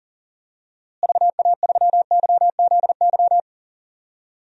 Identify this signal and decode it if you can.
Morse code, heard as a keyed tone, about 37 wpm, 700 Hz. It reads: VA3YZY
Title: DXCC-100 Call Signs - 6 Characters - Repeated 2x - 40wpm